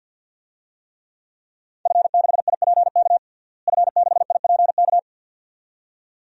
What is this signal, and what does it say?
Morse code, heard as a keyed tone, about 50 wpm, 700 Hz. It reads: F6IPK F6IPK
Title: DXCC-100 Call Signs - 5 Characters - Repeated 2x - 50wpm